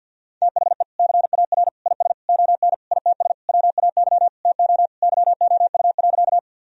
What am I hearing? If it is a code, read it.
Morse code, heard as a keyed tone, about 51 wpm, 700 Hz. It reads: THE CAR IS ON ITS WAY TO YOU.
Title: Sentences from Top 1000 Words - Review Part 1 - 50wpm